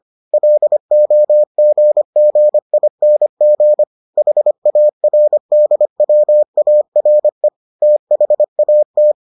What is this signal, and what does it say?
Morse code, heard as a keyed tone, about 25 wpm, 600 Hz. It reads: LOGGING HARDWARE THAT